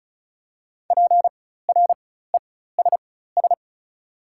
Morse code, heard as a keyed tone, about 36 wpm, spaced as for 16 wpm, 690 Hz. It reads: PRESS
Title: 4x Character Spacing - Top 300 Words - 35wpm